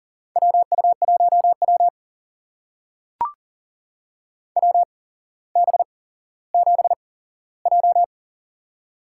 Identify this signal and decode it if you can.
Morse code, heard as a keyed tone, about 41 wpm, spaced as for 17 wpm, 700 Hz. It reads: WU1W WB7J
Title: USA Call Signs - 4 Characters - 8x Character Spacing - 40wpm